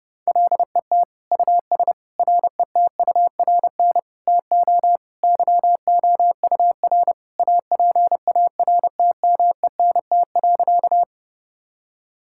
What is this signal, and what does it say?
Morse code, heard as a keyed tone, about 30 wpm, 700 Hz. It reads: LET US RETURN TO YOUR APARTMENT.
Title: Sentences from Top 1000 Words - Review Part 6 - 30wpm